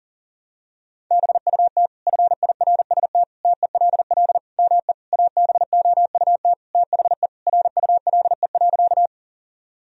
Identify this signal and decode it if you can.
Morse code, heard as a keyed tone, about 40 wpm, 700 Hz. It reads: BUT FIRST TELL ME ABOUT THE RULE.
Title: Sentences from Top 1000 Words - Review Part 2 - 40wpm